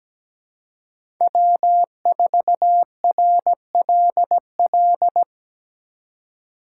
Morse code, heard as a keyed tone, about 20 wpm, 700 Hz. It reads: W4RLL
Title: Speed Racing - USA 1x3 Call Signs 17wpm